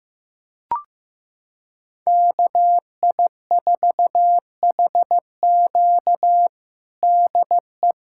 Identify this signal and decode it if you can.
Morse code, heard as a keyed tone, about 15 wpm, 700 Hz. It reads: KI4HQ DE
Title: USA Callsign DE Callsign 15wpm